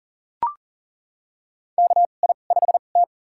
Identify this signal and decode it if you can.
Morse code, heard as a keyed tone, about 40 wpm, spaced as for 29 wpm, 700 Hz. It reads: KI5T
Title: USA Call Signs - 2x Character Spacing - 40wpm